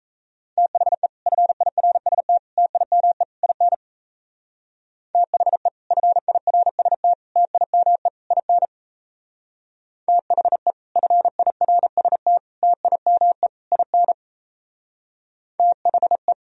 Morse code, heard as a keyed tone, about 33 wpm, 700 Hz. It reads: THE FIRST TIME IN THE FIRST TIME IN THE FIRST TIME IN THE
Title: Speed-Racing - Sets of 4 Words 28wpm